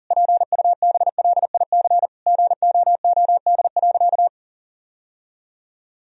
Morse code, heard as a keyed tone, about 37 wpm, 700 Hz. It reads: PUBLIC GOOD.
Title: Sentences from Top 1000 Words - Review Part 3 - 40wpm